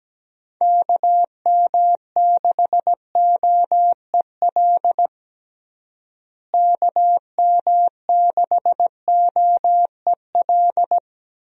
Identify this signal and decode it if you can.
Morse code, heard as a keyed tone, about 17 wpm, 700 Hz. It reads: KM6OEL KM6OEL
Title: USA Call Signs - 6 Characters - Repeated 3x - 17wpm